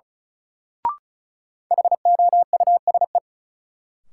Morse code, heard as a keyed tone, about 36 wpm, 700 Hz. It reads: HOUSE
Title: Sets of 2 Words - New Word Drill - 35wpm